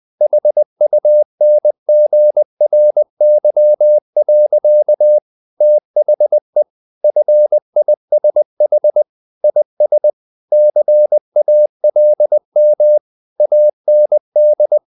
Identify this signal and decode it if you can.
Morse code, heard as a keyed tone, about 20 wpm, 600 Hz. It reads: HUNGRY. THE FISH IS CALM AND
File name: the old man and the sea / the old man and the sea part 1 20wpm